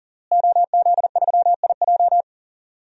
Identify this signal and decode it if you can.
Morse code, heard as a keyed tone, about 40 wpm, 700 Hz. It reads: OZ3IJ
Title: DXCC-100 Call Signs - 5 Characters - Repeated 3x - 40wpm